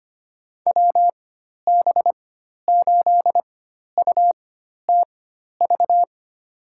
Morse code, heard as a keyed tone, about 25 wpm, spaced as for 12 wpm, 700 Hz. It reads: WB8UTV
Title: USA Call Signs - 6 Characters - 4x Character Spacing - 25wpm